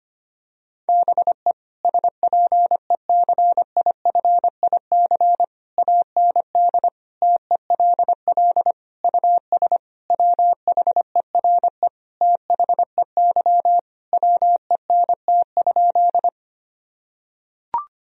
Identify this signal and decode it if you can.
Morse code, heard as a keyed tone, about 25 wpm, 700 Hz. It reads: BE SPECIFIC AND TELL US WHERE THEY WENT?
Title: Sentences from Top 1000 Words - Review Part 3 - 25wpm